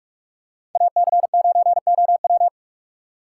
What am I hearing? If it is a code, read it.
Morse code, heard as a keyed tone, about 44 wpm, 700 Hz. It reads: AC9OW
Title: USA Call Signs - 5 Characters - 3x Character Spacing - 45wpm